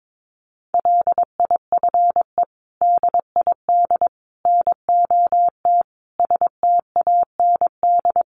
Morse code, heard as a keyed tone, about 22 wpm, 700 Hz. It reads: LIFE DID NOT STAND
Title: Sentences from Top 800 Words 22wpm